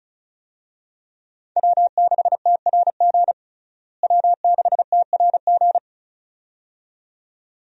Morse code, heard as a keyed tone, about 36 wpm, 700 Hz. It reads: W6TRG W6TRG
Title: USA Call Signs - 5 Characters - Repeated 2x - 35wpm